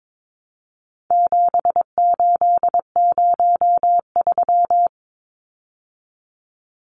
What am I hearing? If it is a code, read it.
Morse code, heard as a keyed tone, about 22 wpm, 700 Hz. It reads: 7803
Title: Numbers - 4-Digit 22wpm